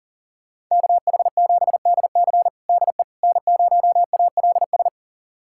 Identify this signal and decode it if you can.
Morse code, heard as a keyed tone, about 37 wpm, 700 Hz. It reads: KH7DC DE N0ALS
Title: USA Callsign DE Callsign - 5 Characters - 40wpm